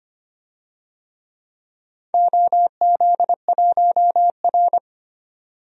Morse code, heard as a keyed tone, about 25 wpm, 700 Hz. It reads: OZ1R
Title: DXCC-100 Call Signs - 4 Characters - 25wpm